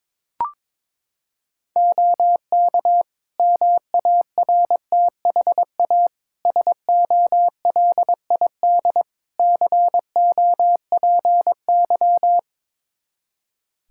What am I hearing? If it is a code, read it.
Morse code, heard as a keyed tone, about 22 wpm, 700 Hz. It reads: OK MARTHA SOLID COPY
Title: Practice QSO - Exchange 3 - More Info - 22wpm